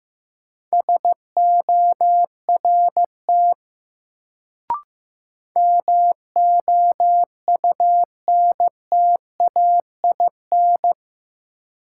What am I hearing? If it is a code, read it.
Morse code, heard as a keyed tone, about 15 wpm, 700 Hz. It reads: SORT MOUNTAIN
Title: Sentences from Top 1300 Words - New Word Drill Alt - 15wpm